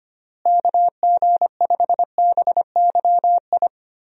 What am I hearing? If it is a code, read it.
Morse code, heard as a keyed tone, about 25 wpm, 700 Hz. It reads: KG5BYI
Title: USA Call Signs - 6 Characters - Repeated 3x - 25wpm